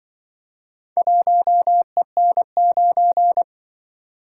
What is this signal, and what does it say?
Morse code, heard as a keyed tone, about 24 wpm, 700 Hz. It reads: 1EN9
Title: Speed-Racing - Four Letter-Number Combinations - 20wpm